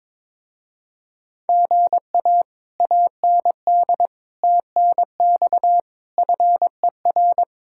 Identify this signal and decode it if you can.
Morse code, heard as a keyed tone, about 22 wpm, 700 Hz. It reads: GA AND TNX FER
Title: Speed-Racing Common QSO Phrases - 22wpm